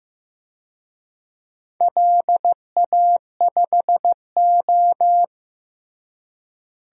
Morse code, heard as a keyed tone, about 16 wpm, 700 Hz. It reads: LA5O
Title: DXCC-100 Call Signs - 4 Characters - Repeated 3x - 15wpm